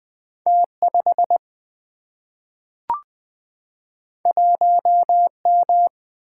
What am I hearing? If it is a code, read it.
Morse code, heard as a keyed tone, about 20 wpm, 700 Hz. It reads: T5 1M
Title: Single-Letter Single-Number 20wpm